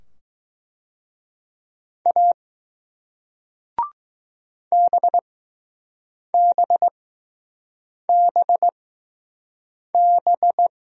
Morse code, heard as a keyed tone, about 20 wpm, spaced as for 6 wpm, 700 Hz. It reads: A BBBB